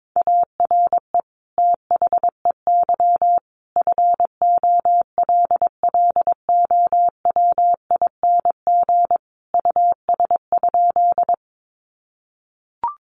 Morse code, heard as a keyed tone, about 22 wpm, 700 Hz. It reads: ARE THEY FOLLOWING US?
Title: Sentences from Top 700 Words 22wpm